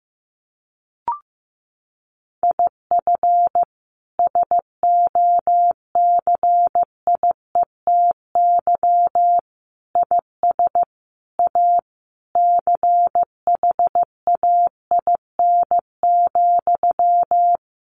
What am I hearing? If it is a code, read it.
Morse code, heard as a keyed tone, about 15 wpm, 700 Hz. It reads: IF SOCIETY IS A CHAIN,
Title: Sentences from Top 1500 Words 15wpm